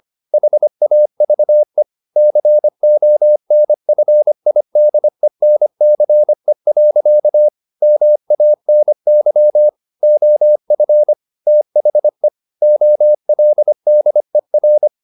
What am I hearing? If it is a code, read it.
Morse code, heard as a keyed tone, about 25 wpm, 600 Hz. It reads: HAVE CONFIDENCE. MANY OF THE OLDER